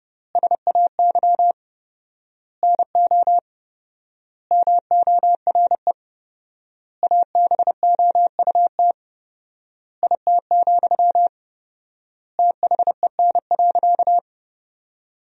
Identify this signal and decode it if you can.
Morse code, heard as a keyed tone, about 32 wpm, 700 Hz. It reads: SAY NO MORE ABOUT IT, THEN.
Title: 4x Word-Spacing - Sentences from Top 200 Words - 30wpm